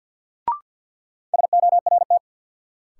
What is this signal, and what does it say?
Morse code, heard as a keyed tone, about 53 wpm, 700 Hz. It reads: SORT
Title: Sets of 2 Words - New Word Drill - 50wpm